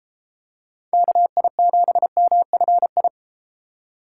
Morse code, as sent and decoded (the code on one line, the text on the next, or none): -.- .. --... -- ..-. ..
KI7MFI